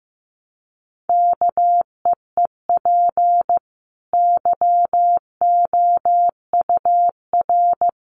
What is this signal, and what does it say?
Morse code, heard as a keyed tone, about 15 wpm, 700 Hz. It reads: KEEP YOUR
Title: Sentences from Top 1200 Words 15wpm